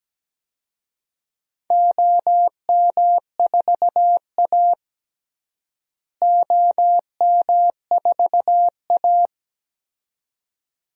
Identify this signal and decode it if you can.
Morse code, heard as a keyed tone, about 17 wpm, 700 Hz. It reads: OM4A OM4A
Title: DXCC-100 Call Signs - 4 Characters - Repeated 2x - 17wpm